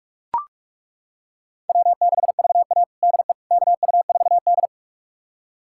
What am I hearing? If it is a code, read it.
Morse code, heard as a keyed tone, about 41 wpm, 700 Hz. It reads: W6VA DE KU4D
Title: USA Callsign DE Callsign - 4 Characters - 45wpm